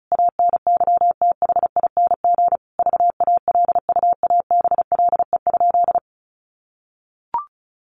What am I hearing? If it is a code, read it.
Morse code, heard as a keyed tone, about 34 wpm, 700 Hz. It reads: ANYTHING VALUABLE?
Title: Sentences from Top 1100 Words 35wpm